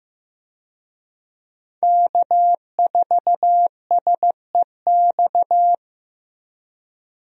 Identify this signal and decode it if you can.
Morse code, heard as a keyed tone, about 15 wpm, 700 Hz. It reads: K4SEX
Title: USA Call Signs - 5 Characters - Repeated 3x - 15wpm